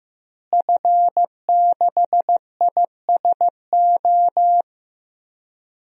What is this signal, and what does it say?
Morse code, heard as a keyed tone, about 15 wpm, 700 Hz. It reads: F6ISO
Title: DXCC-100 Call Signs - 5 Characters - Repeated 3x - 15wpm